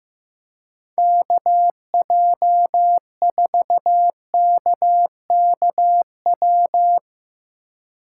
Morse code, heard as a keyed tone, about 16 wpm, 700 Hz. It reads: KJ4KKW
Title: USA Call Signs - 6 Characters - 2x Character Spacing - 15wpm